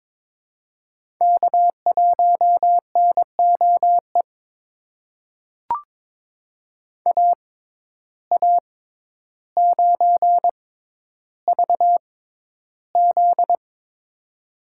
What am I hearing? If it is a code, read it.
Morse code, heard as a keyed tone, about 22 wpm, spaced as for 11 wpm, 700 Hz. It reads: K1NOE AA9VZ
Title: USA Call Signs - 6x Character Spacing - 22wpm